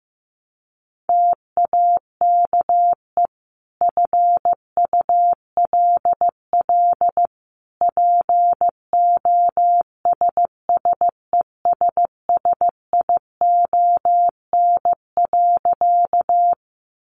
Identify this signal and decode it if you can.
Morse code, heard as a keyed tone, about 15 wpm, 700 Hz. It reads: TAKE FULL POSSESSION.
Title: Sentences from Top 1400 Words 15wpm